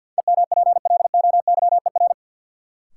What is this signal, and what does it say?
Morse code, heard as a keyed tone, about 46 wpm, 700 Hz. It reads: EMPLOYER
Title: Sentences from Top 1600 Words - New Word Drill Alt - 50wpm